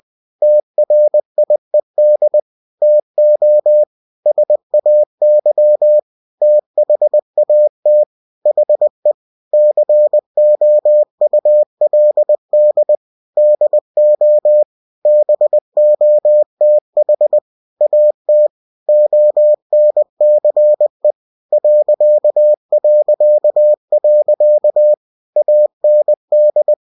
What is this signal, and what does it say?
Morse code, heard as a keyed tone, about 20 wpm, 600 Hz. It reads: TRIED TO SAY THAT HE COULD DO BOTH AT ONCE ... AND
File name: cw-books / winnie-the-pooh / winnie-20wpm-page-0107